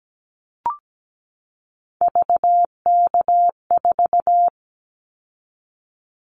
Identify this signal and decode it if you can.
Morse code, heard as a keyed tone, about 17 wpm, 700 Hz. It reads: VK4
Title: Callsign Prefixes for Top 50 DXCC - 17wpm